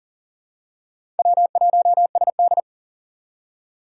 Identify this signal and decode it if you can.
Morse code, heard as a keyed tone, about 40 wpm, 700 Hz. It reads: W1SD
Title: USA Call Signs - 2x Character Spacing - 40wpm